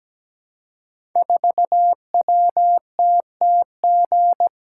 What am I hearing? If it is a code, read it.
Morse code, heard as a keyed tone, about 17 wpm, 700 Hz. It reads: 4WTTG
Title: Code Groups - 5 Letter:Numbers - 17wpm